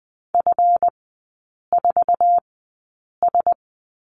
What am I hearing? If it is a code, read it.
Morse code, heard as a keyed tone, about 20 wpm, spaced as for 8 wpm, 690 Hz. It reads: F4S
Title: 205 Lesson 33e - Letter j - Three Characters 20wpm